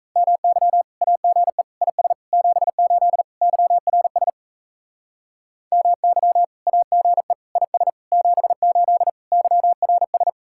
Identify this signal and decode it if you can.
Morse code, heard as a keyed tone, about 35 wpm, 700 Hz. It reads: MY AGE IS 78 YRS MY AGE IS 78 YRS
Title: Speed-Racing Common QSO Phrases - 28wpm